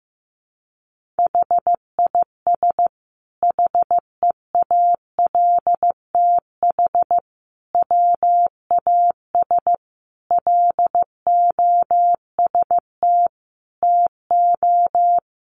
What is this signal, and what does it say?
Morse code, heard as a keyed tone, about 15 wpm, 700 Hz. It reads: HIS HEALTH WAS LOST TO
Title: Sentences from Top 1400 Words 15wpm